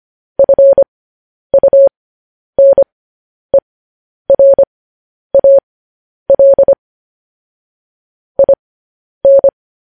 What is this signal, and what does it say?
Morse code, heard as a keyed tone, about 25 wpm, spaced as for 10 wpm, 550 Hz. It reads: FUNERAL IN
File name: podcast / News headlines-1x-25x10